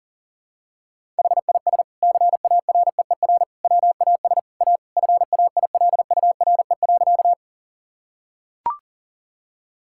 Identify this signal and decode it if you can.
Morse code, heard as a keyed tone, about 40 wpm, 700 Hz. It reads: HIS CAREER WAS A FAILURE.
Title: Sentences from Top 1000 Words - Review Part 6 - 40wpm